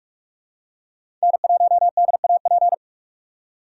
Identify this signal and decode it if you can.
Morse code, heard as a keyed tone, about 46 wpm, 700 Hz. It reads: N1DAP
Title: USA Call Signs - 5 Characters - 4x Character Spacing - 45wpm